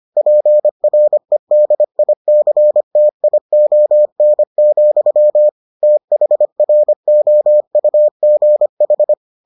Morse code, heard as a keyed tone, about 25 wpm, 600 Hz. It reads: PREDICTION, THROUGH